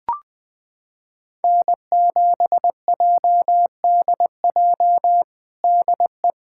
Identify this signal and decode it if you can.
Morse code, heard as a keyed tone, about 20 wpm, 700 Hz. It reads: N7JDJ DE
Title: USA Callsign DE Callsign 20wpm